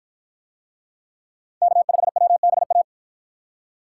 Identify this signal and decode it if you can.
Morse code, heard as a keyed tone, about 50 wpm, 700 Hz. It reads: K5WBA